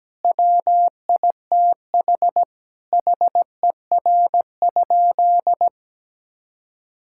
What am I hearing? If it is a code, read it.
Morse code, heard as a keyed tone, about 17 wpm, 700 Hz. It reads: WITH HER?
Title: Sentences from Top 1000 Words - Review Part 6 - 17wpm